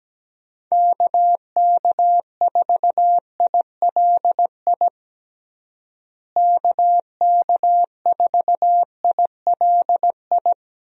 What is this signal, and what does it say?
Morse code, heard as a keyed tone, about 17 wpm, 700 Hz. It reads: KK4ILI KK4ILI
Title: USA Call Signs - 6 Characters - Repeated 3x - 17wpm